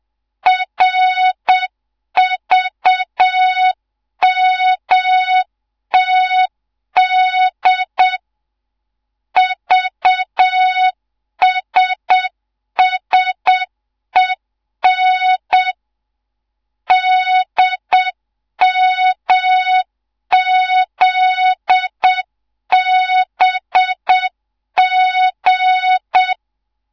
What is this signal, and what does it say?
Morse code, heard as a keyed tone, about 7 wpm, 3700 Hz. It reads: RVMTD VSSEN DMZBG